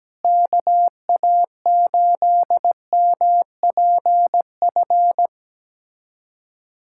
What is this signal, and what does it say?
Morse code, heard as a keyed tone, about 17 wpm, 700 Hz. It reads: KA8MPF
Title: USA Callsign DE Callsign 17wpm